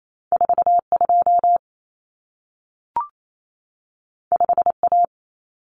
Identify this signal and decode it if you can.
Morse code, heard as a keyed tone, about 28 wpm, 700 Hz. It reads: 42 5A
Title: Common Cut Numbers - 2-Digit 28wpm